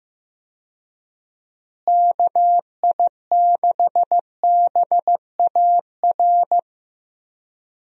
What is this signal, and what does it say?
Morse code, heard as a keyed tone, about 15 wpm, 700 Hz. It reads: KI6BAR
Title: USA Call Signs - 6 Characters - Repeated 2x - 15wpm